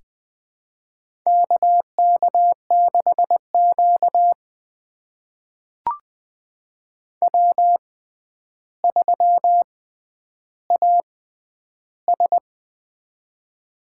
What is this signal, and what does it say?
Morse code, heard as a keyed tone, about 20 wpm, spaced as for 10 wpm, 700 Hz. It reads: KK6Q W3AS
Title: USA Call Signs - 4 Characters - 6x Character Spacing - 20wpm